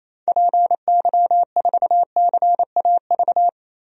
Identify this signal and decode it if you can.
Morse code, heard as a keyed tone, about 28 wpm, 700 Hz. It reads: PY4CAV
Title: DXCC-100 Call Signs - 6 Characters - 28wpm